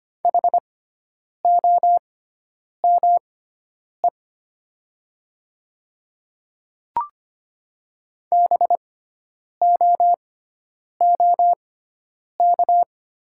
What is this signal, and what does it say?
Morse code, heard as a keyed tone, about 25 wpm, spaced as for 9 wpm, 700 Hz. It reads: HOME BOOK
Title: 6x Character Spacing - Top 300 Words - 25wpm